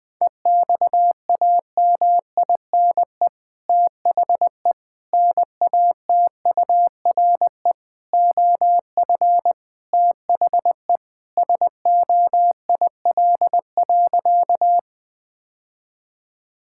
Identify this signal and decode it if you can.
Morse code, heard as a keyed tone, about 21 wpm, 700 Hz. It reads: EXAMINE THE NATURE OF THE SOIL.
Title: Sentences from Top 1600 Words 20wpm